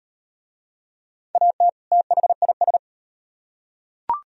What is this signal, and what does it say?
Morse code, heard as a keyed tone, about 35 wpm, 700 Hz. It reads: AT THIS